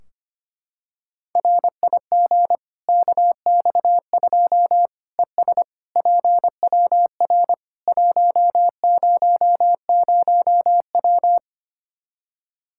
Morse code, heard as a keyed tone, about 25 wpm, 700 Hz. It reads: RIG KX2 ES PWR 100W
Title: Practice QSO - Exchange 2 - Info - 25wpm